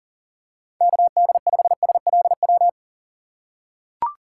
Morse code, heard as a keyed tone, about 40 wpm, 700 Hz. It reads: KD5SLW
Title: USA Call Signs - 6 Characters - 3x Character Spacing - 40wpm